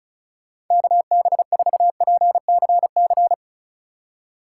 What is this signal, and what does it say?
Morse code, heard as a keyed tone, about 45 wpm, 700 Hz. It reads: KB4PCC